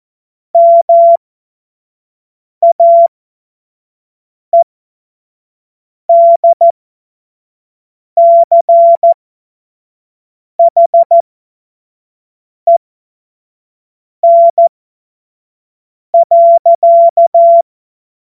Morse code, heard as a keyed tone, about 14 wpm, spaced as for 5 wpm, 680 Hz. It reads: MAEDCHEN.
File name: Morsekurs / Witze / Witz 28.November 2023 25 F70